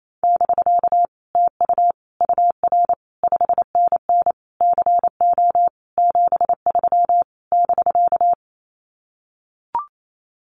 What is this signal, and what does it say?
Morse code, heard as a keyed tone, about 28 wpm, 700 Hz. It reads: <BK> TU UR 5NN CO 73 <BK>
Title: Common QSO Phrases Encore 28wpm